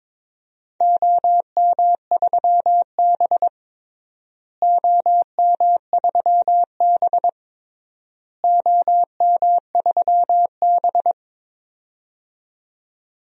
Morse code, heard as a keyed tone, about 22 wpm, 700 Hz. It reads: OM3B OM3B OM3B
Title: DXCC-100 Call Signs - 4 Characters - Repeated 3x - 22wpm